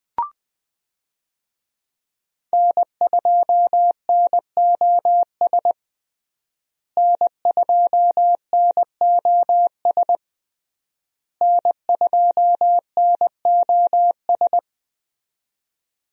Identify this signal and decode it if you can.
Morse code, heard as a keyed tone, about 20 wpm, 700 Hz. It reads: N2NOS N2NOS N2NOS
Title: USA Call Signs - 5 Characters - Repeated 3x - 20wpm